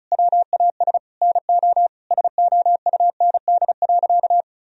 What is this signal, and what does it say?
Morse code, heard as a keyed tone, about 34 wpm, 700 Hz. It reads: WAS NO SOUND.
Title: Sentences from Top 1000 Words - Review Part 2 - 35wpm